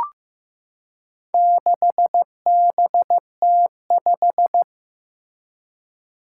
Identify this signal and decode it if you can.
Morse code, heard as a keyed tone, about 15 wpm, 700 Hz. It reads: 6BT5
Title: Four Challenging Letter-Number Combinations 15wpm